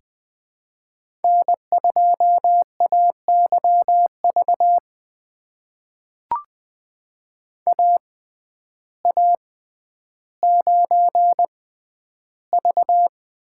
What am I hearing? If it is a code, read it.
Morse code, heard as a keyed tone, about 20 wpm, spaced as for 11 wpm, 700 Hz. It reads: N2AYV AA9V
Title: USA Call Signs - 5 Characters - 6x Character Spacing - 20wpm